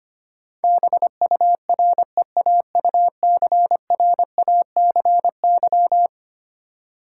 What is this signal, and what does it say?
Morse code, heard as a keyed tone, about 25 wpm, 700 Hz. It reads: BUREAUCRACY
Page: morse-code-ninja-practice-25wpm